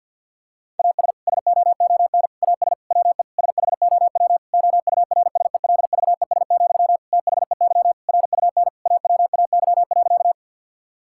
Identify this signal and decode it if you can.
Morse code, heard as a keyed tone, about 50 wpm, 700 Hz. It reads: AS SOON AS WE SHOW OURSELVES, THEY RUN AWAY.
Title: Sentences from Top 1600 Words 50wpm